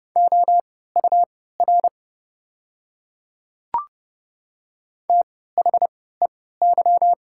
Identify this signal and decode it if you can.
Morse code, heard as a keyed tone, about 30 wpm, spaced as for 17 wpm, 690 Hz. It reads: OUR THEY
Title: 3x Character Spacing - Top 200 Words - 30wpm